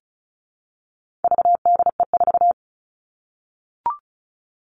Morse code, heard as a keyed tone, about 34 wpm, 700 Hz. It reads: VDE4
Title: All Cut Numbers - 4-Digit 35wpm